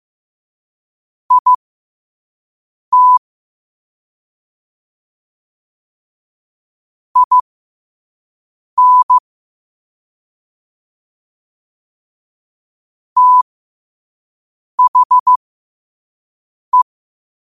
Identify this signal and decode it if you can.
Morse code, heard as a keyed tone, about 15 wpm, spaced as for 5 wpm, 1000 Hz. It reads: IT IN THE
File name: qotdfiles / 5wpm